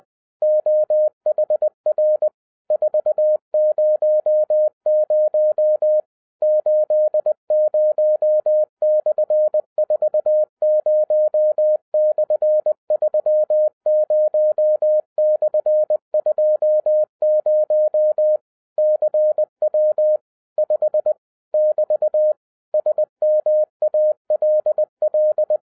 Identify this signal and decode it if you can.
Morse code, heard as a keyed tone, about 20 wpm, 610 Hz. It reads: OHR 400 80/40/30/20 CW 5 <BT> SMALL
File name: 2011-07-mp3 / n23 QRP OTAKU 20wpm